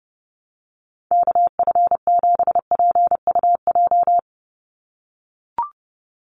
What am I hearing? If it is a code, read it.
Morse code, heard as a keyed tone, about 30 wpm, 700 Hz. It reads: KF7PUJ
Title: USA Call Signs 30wpm